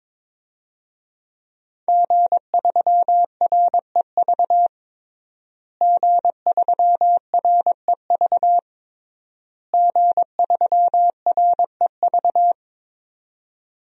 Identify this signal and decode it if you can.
Morse code, heard as a keyed tone, about 22 wpm, 700 Hz. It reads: G3REV G3REV G3REV
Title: DXCC-100 Call Signs - 5 Characters - Repeated 3x - 22wpm